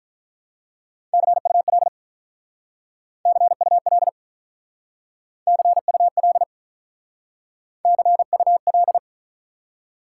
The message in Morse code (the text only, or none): CUL CUL CUL CUL